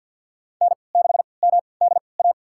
Morse code, heard as a keyed tone, about 47 wpm, spaced as for 28 wpm, 700 Hz. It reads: N6MDA
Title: USA Call Signs - 5 Characters - 3x Character Spacing - 50wpm